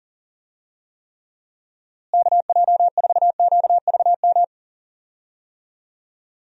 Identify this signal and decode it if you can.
Morse code, heard as a keyed tone, about 39 wpm, 700 Hz. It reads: KJ4QVM
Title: USA Call Signs - 6 Characters - Repeated 2x - 40wpm